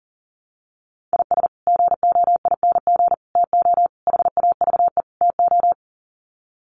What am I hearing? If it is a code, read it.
Morse code, heard as a keyed tone, about 40 wpm, 700 Hz. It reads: IS GOING TO HAVE TO
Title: Sets of 5 Words 40wpm